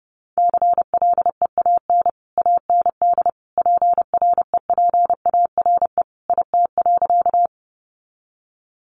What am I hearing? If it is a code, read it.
Morse code, heard as a keyed tone, about 29 wpm, 700 Hz. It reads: CLEAN AND PREPARE IT.
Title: Sentences from Top 1100 Words 30wpm